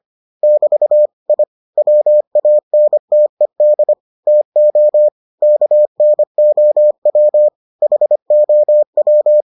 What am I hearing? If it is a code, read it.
Morse code, heard as a keyed tone, about 25 wpm, 600 Hz. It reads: <BT> I WANTED TO KNOW HOW